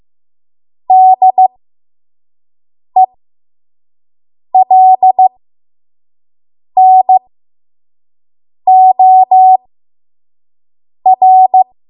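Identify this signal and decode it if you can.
Morse code, heard as a keyed tone, about 15 wpm, spaced as for 5 wpm, 750 Hz. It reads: DELNOR
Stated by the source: 5 WPM Morse code practice.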